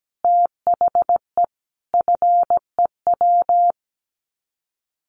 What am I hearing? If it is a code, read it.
Morse code, heard as a keyed tone, about 17 wpm, 700 Hz. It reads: THE FEW
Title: Sets of 5 Words Encore 17wpm